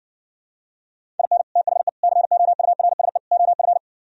Morse code, heard as a keyed tone, about 60 wpm, 700 Hz. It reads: IN THE COURSE OF